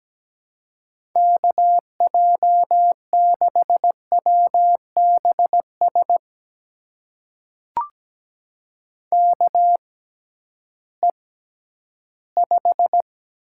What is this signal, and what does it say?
Morse code, heard as a keyed tone, about 17 wpm, spaced as for 11 wpm, 700 Hz. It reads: KJ6WBS KE5
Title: USA Call Signs - 6 Characters - 6x Character Spacing - 17wpm